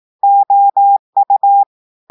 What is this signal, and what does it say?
Morse code, heard as a keyed tone, about 18 wpm, 800 Hz. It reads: OU
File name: audios / portugais / portugais texte en clair 2 vitesse 18 mots minutes